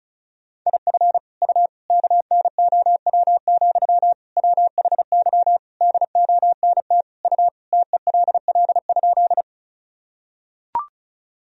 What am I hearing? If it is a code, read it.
Morse code, heard as a keyed tone, about 36 wpm, 700 Hz. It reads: IF U KNOW, WHY DONT U TELL?
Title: Sentences from Top 300 Words and Abbreviations 35wpm